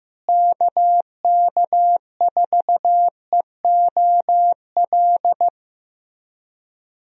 Morse code, heard as a keyed tone, about 15 wpm, 700 Hz. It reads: KK4EOL
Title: USA Call Signs - 6 Characters - Repeated 3x - 15wpm